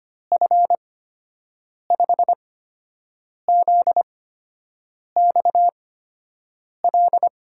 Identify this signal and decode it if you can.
Morse code, heard as a keyed tone, about 25 wpm, spaced as for 7 wpm, 700 Hz. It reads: F5ZXL
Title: USA Call Signs - 6 Characters - 8x Character Spacing - 25wpm